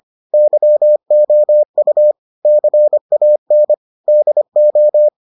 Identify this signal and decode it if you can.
Morse code, heard as a keyed tone, about 25 wpm, 600 Hz. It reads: YOU CAN DO